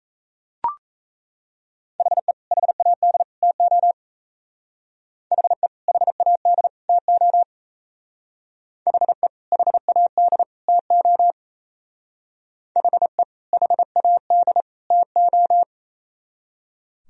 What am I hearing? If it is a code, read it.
Morse code, heard as a keyed tone, about 36 wpm, 700 Hz. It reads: HE HAD TO HE HAD TO HE HAD TO HE HAD TO